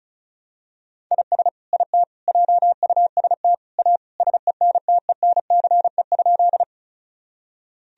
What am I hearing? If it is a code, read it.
Morse code, heard as a keyed tone, about 34 wpm, 700 Hz. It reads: IS IT JUST A SENTENCE?
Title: Sentences from Top 1000 Words - Review Part 1 - 35wpm